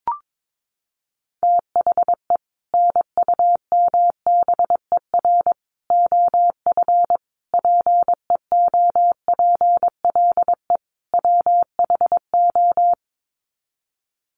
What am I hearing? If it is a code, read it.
Morse code, heard as a keyed tone, about 22 wpm, 700 Hz. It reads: THE NUMBER OF PEOPLE WHO
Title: Sets of 5 Words Encore 22wpm